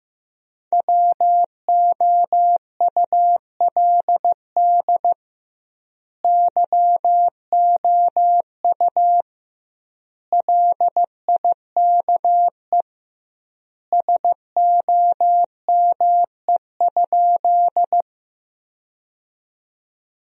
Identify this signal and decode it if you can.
Morse code, heard as a keyed tone, about 16 wpm, 700 Hz. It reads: WOULD YOU LIKE SOME?
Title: 2x Word-Spacing - Sentences from Top 100 Words - 15wpm